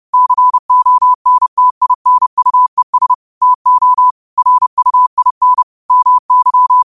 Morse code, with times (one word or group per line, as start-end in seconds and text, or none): 0.1-3.1 CONTINUES
3.4-4.1 TO
4.4-5.6 RUIN
5.9-6.8 MY